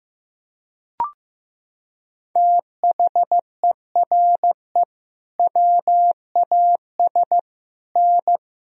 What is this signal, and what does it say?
Morse code, heard as a keyed tone, about 15 wpm, 700 Hz. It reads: THERE WAS N
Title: Sentences from Top 400 Words and Abbreviations 15wpm